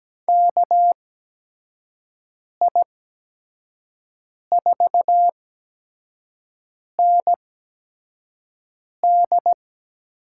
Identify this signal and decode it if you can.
Morse code, heard as a keyed tone, about 17 wpm, spaced as for 5 wpm, 700 Hz. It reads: KI4ND
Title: USA Call Signs - 6 Characters - 8x Character Spacing - 17wpm